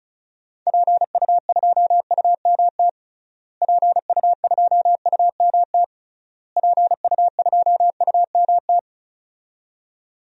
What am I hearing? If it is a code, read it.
Morse code, heard as a keyed tone, about 34 wpm, 700 Hz. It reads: PU2UMT PU2UMT PU2UMT
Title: DXCC-100 Call Signs - 6 Characters - Repeated 3x - 35wpm